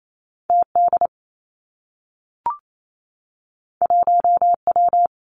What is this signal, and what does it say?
Morse code, heard as a keyed tone, about 28 wpm, 700 Hz. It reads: T D 1 W
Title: Two Challenging Letter-Number Combinations 28wpm